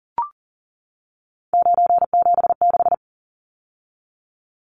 Morse code, heard as a keyed tone, about 40 wpm, 700 Hz. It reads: 976
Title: Numbers - 3-Digit 40wpm